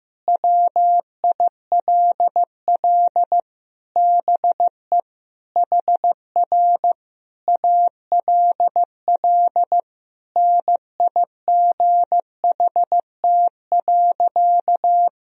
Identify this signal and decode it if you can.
Morse code, heard as a keyed tone, about 15 wpm, 700 Hz. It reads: WILL BE HR ALL NIGHT.
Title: Sentences from Top 300 Words and Abbreviations 15wpm